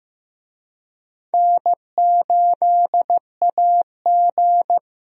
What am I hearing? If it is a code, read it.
Morse code, heard as a keyed tone, about 15 wpm, 700 Hz. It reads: N8AG
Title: USA Call Signs - 4 Characters - Repeated 2x - 15wpm